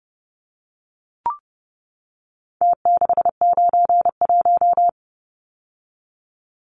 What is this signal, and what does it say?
Morse code, heard as a keyed tone, about 30 wpm, 700 Hz. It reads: T691
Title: Numbers - Contest Style - 4-Digit 30wpm